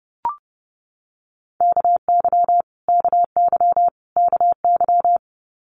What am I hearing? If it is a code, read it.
Morse code, heard as a keyed tone, about 30 wpm, 690 Hz. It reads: KY KY KY
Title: Multi-letter Phonemes Repeated 3x 30wpm